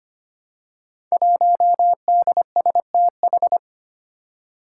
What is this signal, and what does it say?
Morse code, heard as a keyed tone, about 25 wpm, 700 Hz. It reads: 1DSTH
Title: Five Challenging Letter-Number Combinations 25wpm